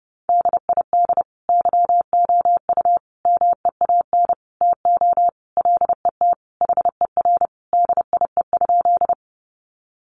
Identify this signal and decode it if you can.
Morse code, heard as a keyed tone, about 30 wpm, 700 Hz. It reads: DID YOU MEAN TO LET HER DIE?
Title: Sentences from Top 1300 Words 30wpm